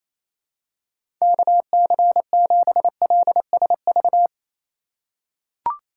Morse code, heard as a keyed tone, about 28 wpm, 700 Hz. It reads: KC7LSV
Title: USA Call Signs - 6 Characters - 4x Character Spacing - 28wpm